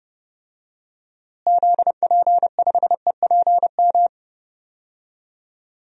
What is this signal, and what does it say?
Morse code, heard as a keyed tone, about 30 wpm, 700 Hz. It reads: ZP5EPM
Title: DXCC-100 Call Signs - 6 Characters - Repeated 2x - 30wpm